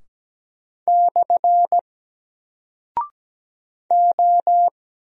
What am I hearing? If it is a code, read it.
Morse code, heard as a keyed tone, about 17 wpm, 700 Hz. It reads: / O
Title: Single Character (Letter:Number:Punctuation:Prosign) 17wpm